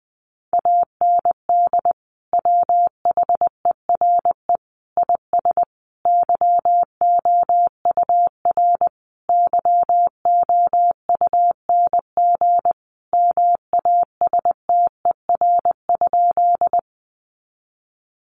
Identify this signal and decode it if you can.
Morse code, heard as a keyed tone, about 20 wpm, 700 Hz. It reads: AND WHERE IS YOUR YOUNG MASTER?
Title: Sentences from Top 1300 Words 20wpm